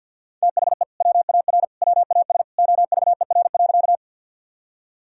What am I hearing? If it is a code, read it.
Morse code, heard as a keyed tone, about 48 wpm, 700 Hz. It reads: THE WAR WAS OVER.
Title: Sentences from Top 1000 Words - Review Part 2 - 50wpm